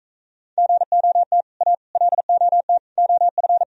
Morse code, heard as a keyed tone, about 42 wpm, 700 Hz. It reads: GOT A LOT OF